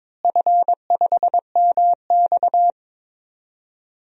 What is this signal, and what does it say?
Morse code, heard as a keyed tone, about 22 wpm, 700 Hz. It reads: F5MX
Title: DXCC-100 Call Signs - 4 Characters - Repeated 3x - 22wpm